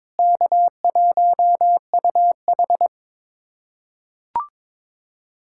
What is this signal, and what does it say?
Morse code, heard as a keyed tone, about 22 wpm, 700 Hz. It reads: K1UH
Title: USA Callsign DE Callsign - 4 Characters - 22wpm